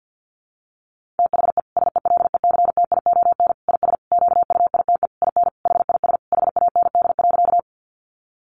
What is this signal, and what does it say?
Morse code, heard as a keyed tone, about 50 wpm, 700 Hz. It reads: THE SELECTION IS QUITE IN HIS HAND.
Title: Sentences from Top 1400 Words 50wpm